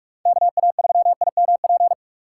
Morse code, heard as a keyed tone, about 46 wpm, 700 Hz. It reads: KA3IMP
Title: USA Call Signs - 6 Characters - 3x Character Spacing - 45wpm